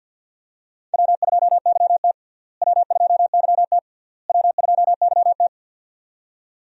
Morse code, heard as a keyed tone, about 50 wpm, 700 Hz. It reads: W2YT W2YT W2YT
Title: USA Call Signs - 4 Characters - Repeated 3x - 50wpm